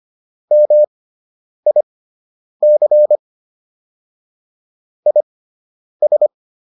Morse code, heard as a keyed tone, about 25 wpm, spaced as for 9 wpm, 600 Hz. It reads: MIC IS